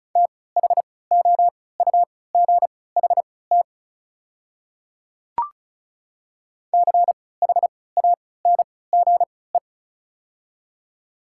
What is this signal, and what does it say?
Morse code, heard as a keyed tone, about 36 wpm, spaced as for 20 wpm, 690 Hz. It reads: THOUGHT CHANGE
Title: 3x Character Spacing - Top 300 Words - 35wpm